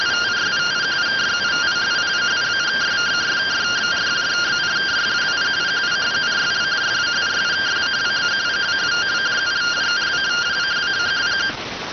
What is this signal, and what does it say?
Two-tone teleprinter signal, about 170 Hz shift.